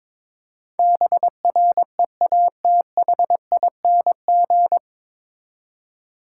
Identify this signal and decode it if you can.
Morse code, heard as a keyed tone, about 21 wpm, 700 Hz. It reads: BREATHING
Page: Morse Code Ninja practice 22wpm